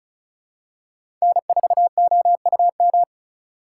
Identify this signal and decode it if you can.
Morse code, heard as a keyed tone, about 36 wpm, 700 Hz. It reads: N4OUM
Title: USA Call Signs - 5 Characters - 4x Character Spacing - 35wpm